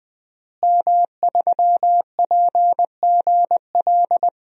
Morse code, heard as a keyed tone, about 20 wpm, 700 Hz. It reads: M3PGL
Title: DXCC-100 Call Signs - 5 Characters - Repeated 2x - 20wpm